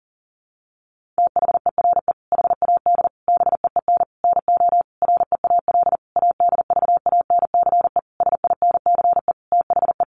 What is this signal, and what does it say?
Morse code, heard as a keyed tone, about 40 wpm, 700 Hz. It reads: THERE HAD BEEN NO REAL ADVANCE SINCE THE
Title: Sentences from Top 1400 Words 40wpm